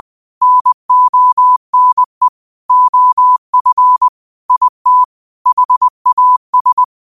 Morse code, heard as a keyed tone, about 20 wpm, 1000 Hz. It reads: NONE OF IT HAS